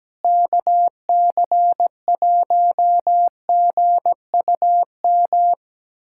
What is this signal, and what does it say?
Morse code, heard as a keyed tone, about 17 wpm, 700 Hz. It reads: KC1GUM
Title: USA Callsign DE Callsign - 6 Characters - 17wpm